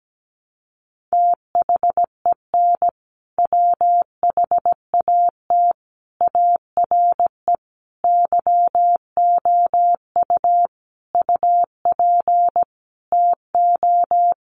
Morse code, heard as a keyed tone, about 17 wpm, 700 Hz. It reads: THEN WHAT ARE YOU UP TO
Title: Sentences from Top 100 Words 17wpm